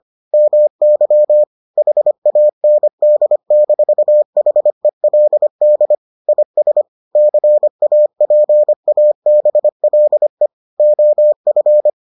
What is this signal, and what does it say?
Morse code, heard as a keyed tone, about 25 wpm, 600 Hz. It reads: MY HAND-HELD IS CAPABLE OF